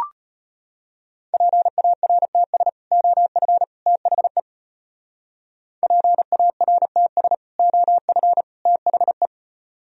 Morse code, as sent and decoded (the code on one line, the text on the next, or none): .--. .- .-. - ... / --- ..-. / - .... / .--. .- .-. - ... / --- ..-. / - .... .
PARTS OF THE PARTS OF THE